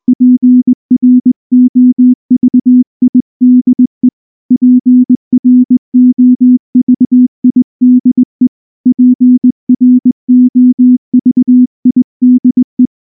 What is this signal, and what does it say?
Morse code, heard as a keyed tone, about 19 wpm, 250 Hz. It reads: PROVIDE PROVIDE PROVIDE
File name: morse / morse-training-20WPM-1